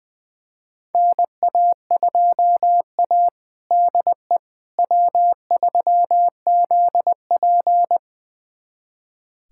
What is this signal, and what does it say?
Morse code, heard as a keyed tone, about 20 wpm, 700 Hz. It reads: NA2A DE W3ZP
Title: USA Callsign DE Callsign - 4 Characters - 20wpm